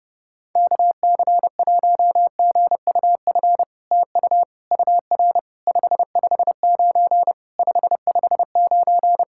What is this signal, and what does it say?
Morse code, heard as a keyed tone, about 32 wpm, 700 Hz. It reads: KC1GUF TU UR 559 559
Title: SOTA QSO Practice 30wpm